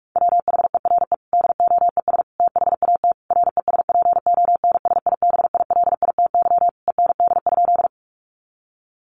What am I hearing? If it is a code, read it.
Morse code, heard as a keyed tone, about 44 wpm, 700 Hz. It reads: WHERE DOES THAT RESPONSIBILITY END?
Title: Sentences from Top 1200 Words 45wpm